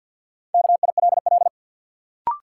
Morse code, heard as a keyed tone, about 53 wpm, 700 Hz. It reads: KILL
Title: Sentences from Top 1400 Words - New Word Drill - 50wpm